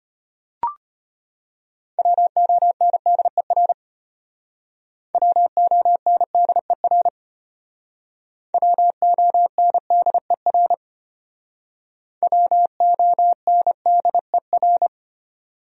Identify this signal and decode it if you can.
Morse code, heard as a keyed tone, about 34 wpm, 700 Hz. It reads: WONDER WONDER WONDER WONDER